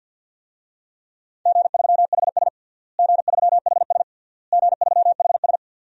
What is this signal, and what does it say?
Morse code, heard as a keyed tone, about 52 wpm, 700 Hz. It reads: G3HS G3HS G3HS
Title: DXCC-100 Call Signs - 4 Characters - Repeated 3x - 50wpm